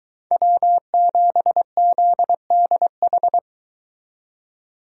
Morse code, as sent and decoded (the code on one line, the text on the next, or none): .-- --... --.. -.. ....
W7ZDH